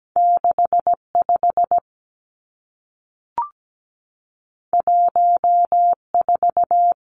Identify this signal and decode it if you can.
Morse code, heard as a keyed tone, about 17 wpm, 700 Hz. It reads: 65 14
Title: Two Challenging Letter-Number Combinations 17wpm